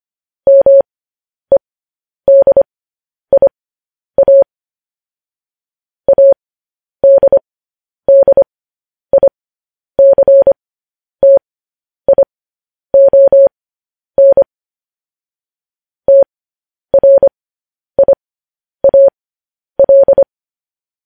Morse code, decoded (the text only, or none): MEDIA ADDICTION TRIAL